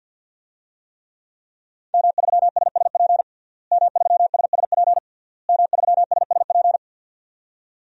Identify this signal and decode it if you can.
Morse code, heard as a keyed tone, about 51 wpm, 700 Hz. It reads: M3SSP M3SSP M3SSP
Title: DXCC-100 Call Signs - 5 Characters - Repeated 3x - 50wpm